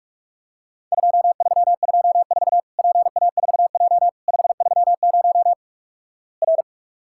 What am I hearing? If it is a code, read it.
Morse code, heard as a keyed tone, about 46 wpm, 700 Hz. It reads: 2324 PA4J 530 R
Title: Worked All Europe DX Contest - QTC Exchange - 45wpm